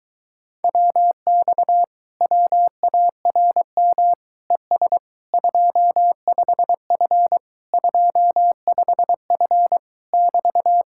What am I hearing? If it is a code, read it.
Morse code, heard as a keyed tone, about 24 wpm, 700 Hz. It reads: WX WARM ES 25F 25F <BT>